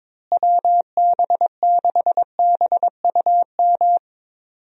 Morse code, heard as a keyed tone, about 22 wpm, 700 Hz. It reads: WB6BUM
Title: USA Call Signs - 6 Characters - Repeated 2x - 22wpm